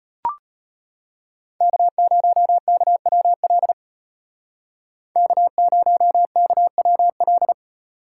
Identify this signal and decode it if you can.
Morse code, heard as a keyed tone, about 32 wpm, 700 Hz. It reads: K0KWL K0KWL